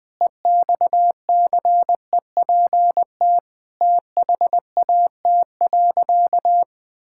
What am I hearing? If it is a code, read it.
Morse code, heard as a keyed tone, about 20 wpm, 700 Hz. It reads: EXCEPT THAT.
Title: Sentences from Top 1600 Words 20wpm